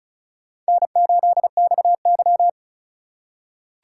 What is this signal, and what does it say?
Morse code, heard as a keyed tone, about 36 wpm, 700 Hz. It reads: N8XY
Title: USA Call Signs - 4 Characters - 4x Character Spacing - 35wpm